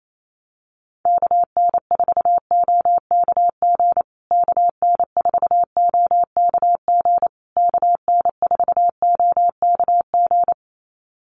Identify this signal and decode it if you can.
Morse code, heard as a keyed tone, about 28 wpm, 700 Hz. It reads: KN4OKG KN4OKG KN4OKG
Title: USA Call Signs Repeated 3x 28wpm